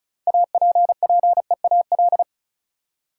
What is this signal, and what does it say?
Morse code, heard as a keyed tone, about 34 wpm, 700 Hz. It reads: APPEAL
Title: Six Letter Words - 35wpm